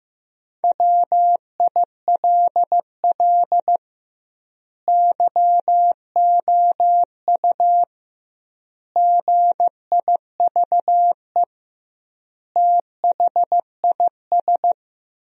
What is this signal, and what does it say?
Morse code, heard as a keyed tone, about 15 wpm, 700 Hz. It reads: WILL YOU GIVE THIS
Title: 2x Word-Spacing - Sentences from Top 200 Words - 15wpm